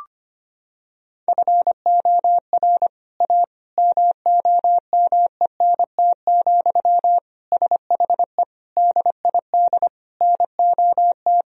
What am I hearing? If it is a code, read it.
Morse code, heard as a keyed tone, about 25 wpm, 700 Hz. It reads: FOR A MOMENT, SHE DID NOT
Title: Sentences from Top 1000 Words - Review Part 2 - 25wpm